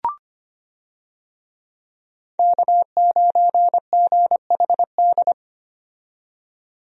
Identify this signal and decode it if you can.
Morse code, heard as a keyed tone, about 25 wpm, 700 Hz. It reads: K9GHD
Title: USA Call Signs - 5 Characters - 25wpm